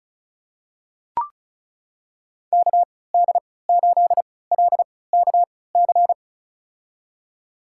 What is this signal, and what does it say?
Morse code, heard as a keyed tone, about 36 wpm, spaced as for 20 wpm, 700 Hz. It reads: KD8LKC
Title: USA Call Signs - 6 Characters - 3x Character Spacing - 35wpm